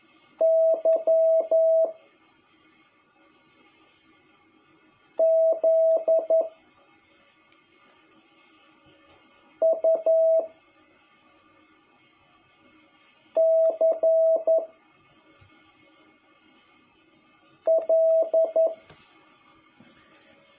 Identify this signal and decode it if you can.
Morse code, heard as a keyed tone, about 11 wpm, 660 Hz. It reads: Y Z U C L
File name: jh8cbh / 2021cw